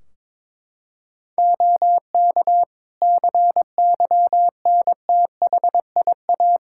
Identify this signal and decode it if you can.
Morse code, heard as a keyed tone, about 22 wpm, 700 Hz. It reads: OK CYNTHIA
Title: K1USN SST QSO Practice 22wpm